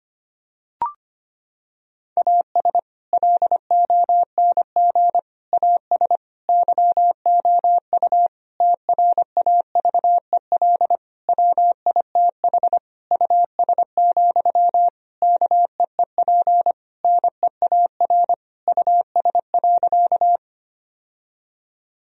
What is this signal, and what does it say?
Morse code, heard as a keyed tone, about 25 wpm, 700 Hz. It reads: AS LONG AS YOU TRAVEL WITH US, KEEP NEAR US.
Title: Sentences from Top 1000 Words - Review Part 2 - 25wpm